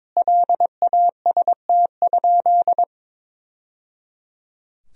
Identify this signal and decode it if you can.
Morse code, heard as a keyed tone, about 22 wpm, 700 Hz. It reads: LAST?
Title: Sentences from Top 1000 Words - Review Part 4 - 22wpm